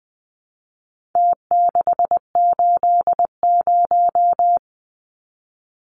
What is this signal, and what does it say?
Morse code, heard as a keyed tone, about 20 wpm, 700 Hz. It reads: T680
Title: Numbers - Contest Style - 4-Digit 20wpm